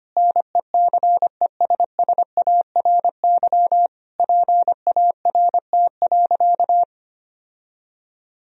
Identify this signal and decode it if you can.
Morse code, heard as a keyed tone, about 25 wpm, 700 Hz. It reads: NECESSARY PART.
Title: Sentences from Top 1000 Words - Review Part 1 - 25wpm